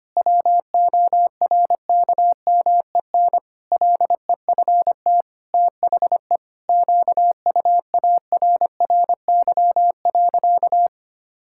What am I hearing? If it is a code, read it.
Morse code, heard as a keyed tone, about 25 wpm, 700 Hz. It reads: WORKMEN LEFT THE QUARRY.
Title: Pangram Challenge v2 - 25wpm